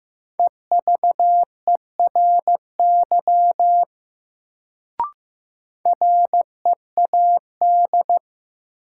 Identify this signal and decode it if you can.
Morse code, heard as a keyed tone, about 16 wpm, 700 Hz. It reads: EVERY READ
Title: Sentences from Top 200 Words - New Word Drill - 15wpm